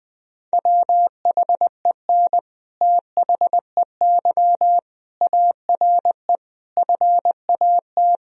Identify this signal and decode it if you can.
Morse code, heard as a keyed tone, about 21 wpm, 700 Hz. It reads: WHEN THEY ARE FAT
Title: Sentences from Top 1000 Words - Review Part 4 - 20wpm